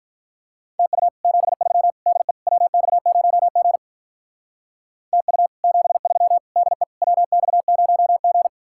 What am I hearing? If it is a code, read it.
Morse code, heard as a keyed tone, about 46 wpm, 700 Hz. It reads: TU 73 DE WX0G TU 73 DE WX0G